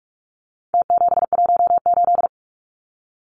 Morse code, heard as a keyed tone, about 44 wpm, 700 Hz. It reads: T718
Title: Numbers - Contest Style - 4-Digit 45wpm